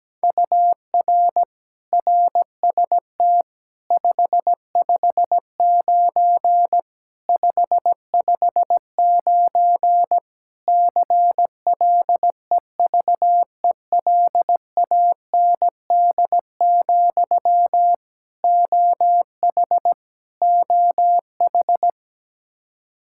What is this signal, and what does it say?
Morse code, heard as a keyed tone, about 17 wpm, 700 Hz. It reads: UR RST 559 559 CLEVELAND, OH OH
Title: Practice QSO - Exchange 1 - Beginning - 17wpm